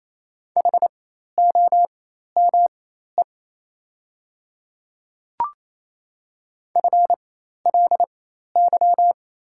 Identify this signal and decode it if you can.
Morse code, heard as a keyed tone, about 27 wpm, spaced as for 13 wpm, 690 Hz. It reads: HOME FLY
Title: 4x Character Spacing - Top 500 Words - 28wpm